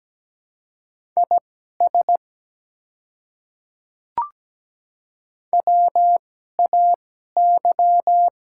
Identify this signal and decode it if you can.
Morse code, heard as a keyed tone, about 17 wpm, spaced as for 12 wpm, 690 Hz. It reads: IS WAY
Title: 2x Character Spacing - Top 100 Words - 17wpm